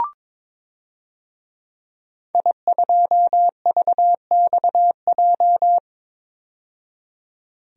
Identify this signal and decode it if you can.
Morse code, heard as a keyed tone, about 22 wpm, 700 Hz. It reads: I2VXJ
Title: DXCC-100 Call Signs - 5 Characters - 22wpm